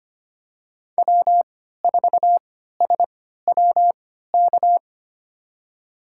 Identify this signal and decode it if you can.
Morse code, heard as a keyed tone, about 25 wpm, spaced as for 14 wpm, 700 Hz. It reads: W4SWK
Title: USA Call Signs - 5 Characters - 3x Character Spacing - 25wpm